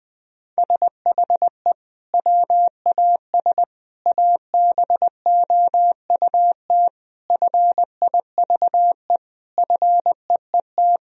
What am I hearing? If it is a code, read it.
Morse code, heard as a keyed tone, about 19 wpm, 700 Hz. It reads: SHE WAS ABOUT FIVE FEET
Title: Sentences from Top 1000 Words - Review Part 2 - 20wpm